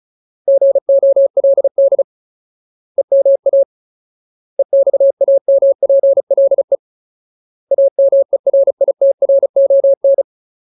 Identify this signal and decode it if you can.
Morse code, heard as a keyed tone, about 34 wpm, 550 Hz. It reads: GOLD EMA EXAMPLE AMERITRON
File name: Ultimate Word List / 35WPM Ultimate